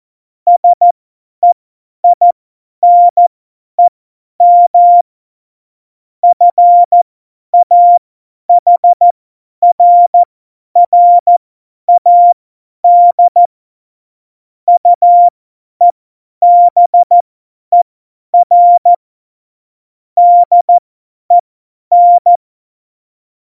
Morse code, heard as a keyed tone, about 14 wpm, spaced as for 10 wpm, 700 Hz. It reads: SEINEM FAHRRAD UEBER DEN